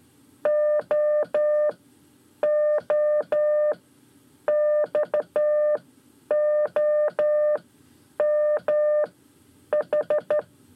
Morse code, heard as a keyed tone, about 13 wpm, spaced as for 8 wpm, 600 Hz. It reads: OOXOMH
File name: 5 Lezione CW 1